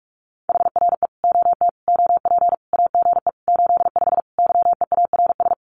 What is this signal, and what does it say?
Morse code, heard as a keyed tone, about 44 wpm, 700 Hz. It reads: HRE OT OP AGE 85 YEARS
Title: Common QSO Phrases 45wpm